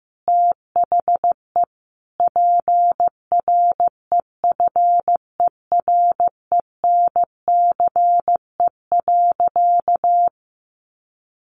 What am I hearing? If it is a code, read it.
Morse code, heard as a keyed tone, about 15 wpm, 700 Hz. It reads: THE PREFERENCE.
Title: Sentences from Top 1500 Words 15wpm